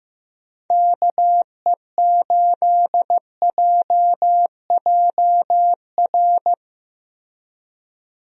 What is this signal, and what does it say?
Morse code, heard as a keyed tone, about 15 wpm, 700 Hz. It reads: KE8JJR
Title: USA Call Signs - 6 Characters - Repeated 3x - 15wpm